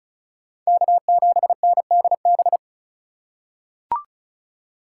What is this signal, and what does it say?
Morse code, heard as a keyed tone, about 36 wpm, 700 Hz. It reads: K7NDB
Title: USA Call Signs - 5 Characters - 3x Character Spacing - 35wpm